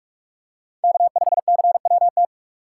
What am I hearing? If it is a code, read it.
Morse code, heard as a keyed tone, about 42 wpm, 700 Hz. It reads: K5CWT